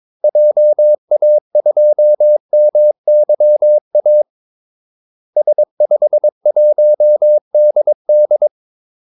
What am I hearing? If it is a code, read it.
Morse code, heard as a keyed tone, about 22 wpm, 600 Hz. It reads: JA2MYA S51DD
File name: Callsigns-22wpm